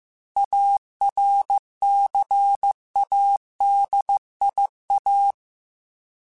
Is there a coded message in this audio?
Yes: Morse code from 0.4 to 5.3 s, heard as a keyed tone, about 15 wpm, 800 Hz. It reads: ARCADIA